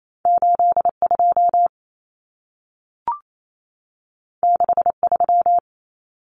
Morse code, heard as a keyed tone, about 28 wpm, 700 Hz. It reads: 82 63
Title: Common Cut Numbers - 2-Digit 28wpm